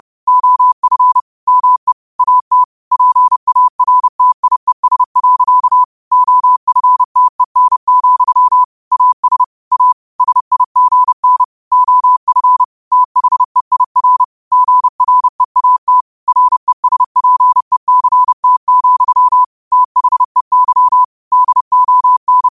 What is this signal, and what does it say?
Morse code, heard as a keyed tone, about 30 wpm, 1000 Hz. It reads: OF ME AT PARTIES. OFTEN, AS A SIGN OF THEIR GREAT RESPECT, THEY DON
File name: qotdfiles / 30wpm